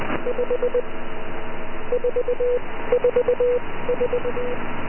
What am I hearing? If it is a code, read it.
Morse code, heard as a keyed tone, about 20 wpm, spaced as for 15 wpm, 490 Hz, with noise. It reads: 5 444